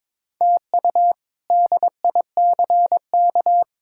Morse code, heard as a keyed tone, about 22 wpm, 700 Hz. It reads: TU DICK
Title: K1USN SST QSO Practice - 22wpm